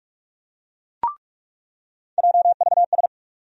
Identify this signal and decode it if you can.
Morse code, heard as a keyed tone, about 43 wpm, 700 Hz. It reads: JVS
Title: USA Call Sign Postfixes - 3 Letters 45wpm